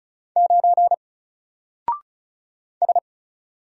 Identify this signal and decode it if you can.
Morse code, heard as a keyed tone, about 33 wpm, 700 Hz. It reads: 9 S
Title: Single Letter-Number Alt 35wpm